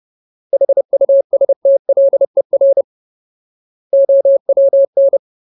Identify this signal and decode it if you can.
Morse code, heard as a keyed tone, about 30 wpm, 550 Hz. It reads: HUSTLER OWN
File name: Ultimate Word List / 30WPM Ultimate